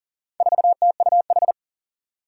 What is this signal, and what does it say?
Morse code, heard as a keyed tone, about 37 wpm, 690 Hz. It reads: 4TUH
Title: Four Challenging Letter-Number Combinations 40wpm